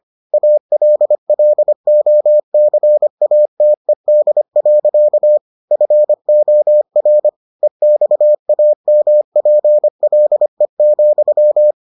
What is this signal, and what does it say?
Morse code, heard as a keyed tone, about 25 wpm, 600 Hz. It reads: ALLOCATED. FOR EXAMPLE,